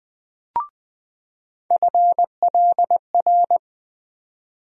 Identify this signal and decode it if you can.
Morse code, heard as a keyed tone, about 20 wpm, 700 Hz. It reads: FLR
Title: USA Call Sign Postfixes - 3 Letters 20wpm